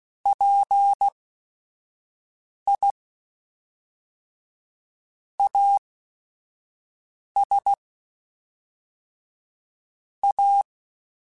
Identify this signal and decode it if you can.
Morse code, heard as a keyed tone, about 16 wpm, spaced as for 5 wpm, 800 Hz. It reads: PI AS A
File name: Technical Subjects / Technical Subjects-mp3s / XL @5